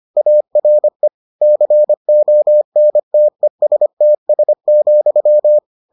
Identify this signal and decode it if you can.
Morse code, heard as a keyed tone, about 25 wpm, 600 Hz. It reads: ARE CONTESTS,